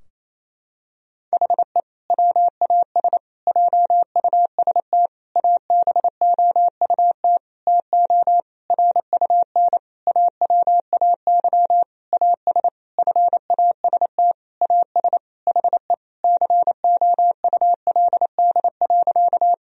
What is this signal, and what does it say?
Morse code, heard as a keyed tone, about 28 wpm, 700 Hz. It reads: HE WAS JUST ABOUT TO RUN AWAY AS FAST AS HE COULD.
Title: Sentences from Top 1000 Words - Review Part 2 - 28wpm